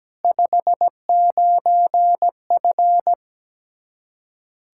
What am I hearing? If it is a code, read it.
Morse code, heard as a keyed tone, about 17 wpm, 700 Hz. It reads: 59F
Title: K1USN SST QSO Practice 17wpm